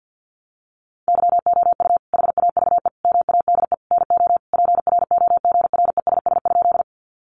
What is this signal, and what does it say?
Morse code, heard as a keyed tone, about 49 wpm, 700 Hz. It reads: YOU HAVE MADE NO PROGRESS?
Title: Sentences from Top 1400 Words 50wpm